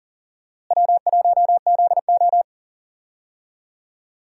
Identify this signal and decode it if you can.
Morse code, heard as a keyed tone, about 37 wpm, 700 Hz. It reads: W1ZO
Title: USA Call Signs - 4 Characters - Repeated 2x - 40wpm